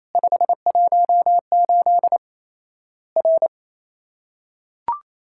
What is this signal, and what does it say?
Morse code, heard as a keyed tone, about 28 wpm, 700 Hz. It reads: 518 R
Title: Worked All Europe DX Contest - QTC Exchange - 28wpm